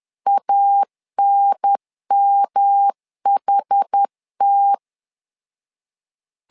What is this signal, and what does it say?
Morse code, heard as a keyed tone, about 10 wpm, 800 Hz. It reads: ANMHT
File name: LEZIONI / 4 Q 10wpm 2210